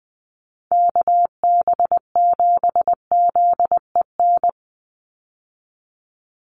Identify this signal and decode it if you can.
Morse code, heard as a keyed tone, about 20 wpm, 700 Hz. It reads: KB7ZEN
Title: USA Call Signs 20wpm